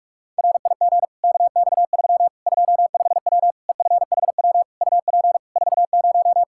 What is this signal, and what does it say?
Morse code, heard as a keyed tone, about 46 wpm, 700 Hz. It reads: RIG KX3 25W EFHW UP 40
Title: Practice QSO - Exchange 2 - Info - 45wpm